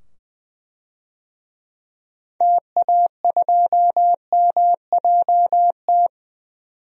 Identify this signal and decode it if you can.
Morse code, heard as a keyed tone, about 20 wpm, 700 Hz. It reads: TA2MJT
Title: DXCC-100 Call Signs - 6 Characters - 20wpm